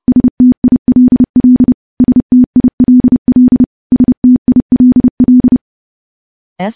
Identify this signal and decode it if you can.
Morse code, heard as a keyed tone, about 28 wpm, 250 Hz. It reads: STILL STILL STILL
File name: morse / morse-training-30WPM